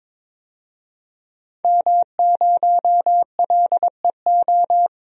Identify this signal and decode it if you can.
Morse code, heard as a keyed tone, about 22 wpm, 700 Hz. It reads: M0LEO
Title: DXCC-100 Call Signs - 5 Characters - 22wpm